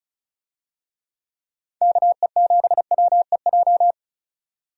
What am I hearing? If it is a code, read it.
Morse code, heard as a keyed tone, about 36 wpm, 700 Hz. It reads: KE7WEJ
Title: USA Call Signs - 6 Characters - 35wpm